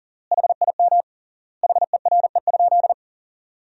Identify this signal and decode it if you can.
Morse code, heard as a keyed tone, about 40 wpm, 700 Hz. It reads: HIM HERE?
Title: 3x Word-Spacing - Sentences from Top 500 Words - 40wpm